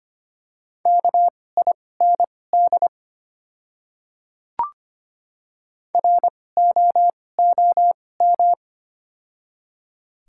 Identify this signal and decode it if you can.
Morse code, heard as a keyed tone, about 25 wpm, spaced as for 18 wpm, 690 Hz. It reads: KIND ROOM
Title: 2x Character Spacing - Top 300 Words - 25wpm